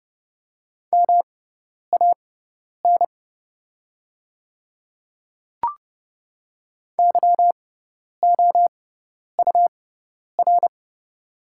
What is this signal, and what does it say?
Morse code, heard as a keyed tone, about 30 wpm, spaced as for 10 wpm, 700 Hz. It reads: MAN YOUR
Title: 6x Character Spacing - Top 200 Words - 30wpm